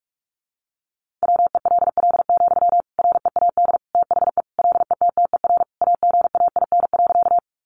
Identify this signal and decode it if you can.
Morse code, heard as a keyed tone, about 44 wpm, 700 Hz. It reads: WELL, READ THE LETTER AGAIN.
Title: Sentences from Top 300 Words 45wpm